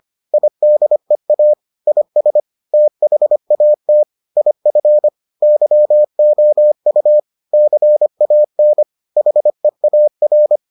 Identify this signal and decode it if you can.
Morse code, heard as a keyed tone, about 25 wpm, 600 Hz. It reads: IDEA IS THAT IF YOU CAN HEAR